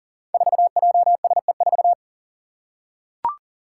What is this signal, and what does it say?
Morse code, heard as a keyed tone, about 37 wpm, 700 Hz. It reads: 4JSE4
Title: Five Challenging Letter-Number Combinations 40wpm